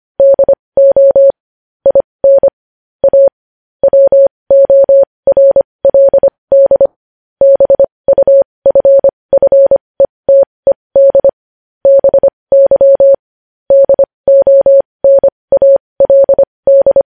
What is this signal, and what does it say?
Morse code, heard as a keyed tone, about 25 wpm, spaced as for 20 wpm, 550 Hz. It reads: DO IN A WORLD BUFFETED BY DONALD
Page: news headlines in morse code at 20 wpm